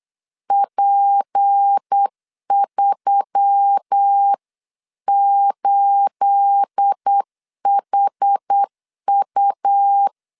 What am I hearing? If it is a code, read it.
Morse code, heard as a keyed tone, about 8 wpm, 800 Hz. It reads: P38HU
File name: CW / DET / 1 9 1500C S C 8wpm 5653